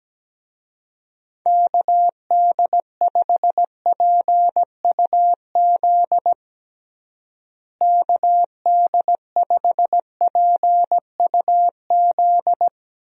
Morse code, heard as a keyed tone, about 17 wpm, 700 Hz. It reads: KD5PUZ KD5PUZ
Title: USA Call Signs - 6 Characters - Repeated 2x - 17wpm